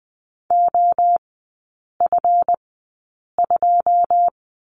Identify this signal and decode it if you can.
Morse code, heard as a keyed tone, about 20 wpm, spaced as for 8 wpm, 700 Hz. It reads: OF2
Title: 138 Lesson 23e - Letter f - Three Characters 20wpm